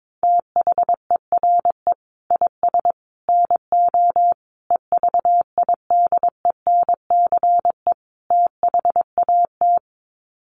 Morse code, heard as a keyed tone, about 22 wpm, 700 Hz. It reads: THERE IS NO EVIDENCE THAT
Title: Sets of 5 Words Encore 22wpm